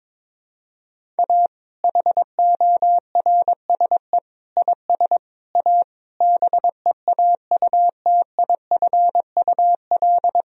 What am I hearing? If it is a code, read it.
Morse code, heard as a keyed tone, about 22 wpm, 700 Hz. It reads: A HORSE IS A BEAUTIFUL
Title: Sentences from Top 1000 Words - Review Part 1 - 22wpm